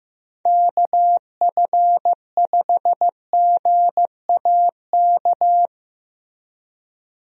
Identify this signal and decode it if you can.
Morse code, heard as a keyed tone, about 15 wpm, 700 Hz. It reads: KF5GAK
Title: USA Call Signs - 6 Characters - Repeated 3x - 15wpm